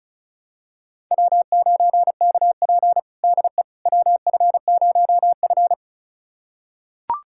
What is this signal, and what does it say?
Morse code, heard as a keyed tone, about 36 wpm, 700 Hz. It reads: W9KP DE WF0F
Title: USA Callsign DE Callsign - 4 Characters - 35wpm